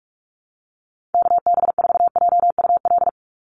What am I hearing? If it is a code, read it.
Morse code, heard as a keyed tone, about 44 wpm, 700 Hz. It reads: KB4JUL
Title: USA Call Signs 45wpm